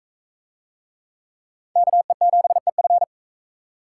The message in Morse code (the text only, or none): KE7EF